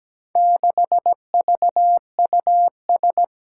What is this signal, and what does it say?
Morse code, heard as a keyed tone, about 17 wpm, 700 Hz. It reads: 6VUS
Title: Four Challenging Letter-Number Combinations 17wpm